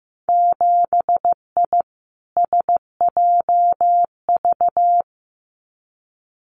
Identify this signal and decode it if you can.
Morse code, heard as a keyed tone, about 15 wpm, 700 Hz. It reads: 7I SJV
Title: Winter Field Day Exchanges Encore 15wpm